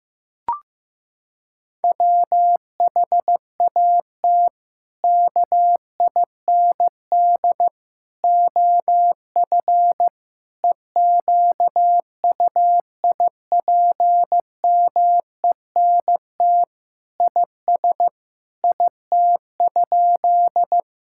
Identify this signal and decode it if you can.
Morse code, heard as a keyed tone, about 16 wpm, 700 Hz. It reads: WHAT KIND OF EQUIPMENT IS IT?
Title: Sentences from Top 1000 Words - Review Part 5 - 15wpm